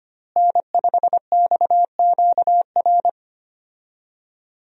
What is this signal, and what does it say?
Morse code, heard as a keyed tone, about 25 wpm, 700 Hz. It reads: N5XQR
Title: USA Call Signs - 5 Characters - Repeated 2x - 25wpm